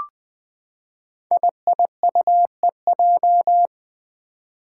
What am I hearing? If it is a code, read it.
Morse code, heard as a keyed tone, about 20 wpm, 700 Hz. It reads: IIUEJ
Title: Five Challenging Letter-Number Combinations 20wpm